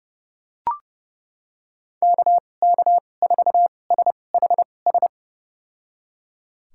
Morse code, heard as a keyed tone, about 32 wpm, spaced as for 22 wpm, 700 Hz. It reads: KK4SHS
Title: USA Call Signs - 6 Characters - 2x Character Spacing - 30wpm